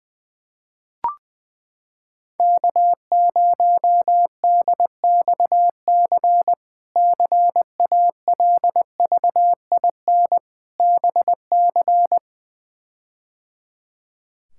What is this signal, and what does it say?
Morse code, heard as a keyed tone, about 21 wpm, 700 Hz. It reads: K0DXC CALVIN BC
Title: K1USN SST QSO Practice - 20wpm